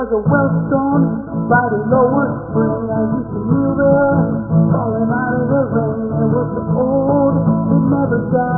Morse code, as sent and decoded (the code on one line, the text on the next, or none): none